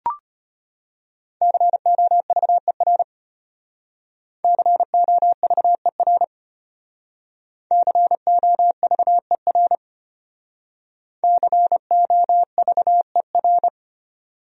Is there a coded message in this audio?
Yes: Morse code from 1.4 to 13.7 s, heard as a keyed tone, about 33 wpm, 700 Hz. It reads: COVER COVER COVER COVER